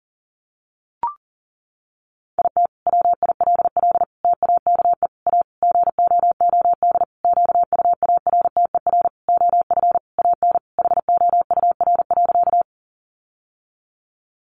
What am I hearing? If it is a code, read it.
Morse code, heard as a keyed tone, about 37 wpm, 700 Hz. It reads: IT WILL TAKE A GOOD QUARTER OF AN HOUR.
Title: Sentences from Top 1300 Words 40wpm